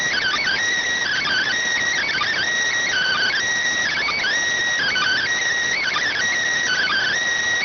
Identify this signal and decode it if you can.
Emergency Alert System (SAME) tones (header bursts), heard as harsh buzzing bursts. It is heard from 1.0 to 7.1 s.